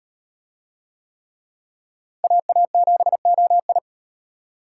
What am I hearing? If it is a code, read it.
Morse code, heard as a keyed tone, about 38 wpm, 700 Hz. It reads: AA7OI